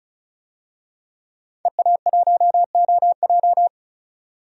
Morse code, heard as a keyed tone, about 36 wpm, 700 Hz. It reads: EA1OJ
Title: DXCC-100 Call Signs - 5 Characters - 35wpm